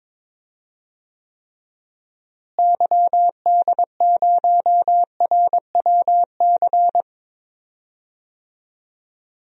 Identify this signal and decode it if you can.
Morse code, heard as a keyed tone, about 21 wpm, 700 Hz. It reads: YD0RWC
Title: DXCC-100 Call Signs - 6 Characters - Repeated 2x - 22wpm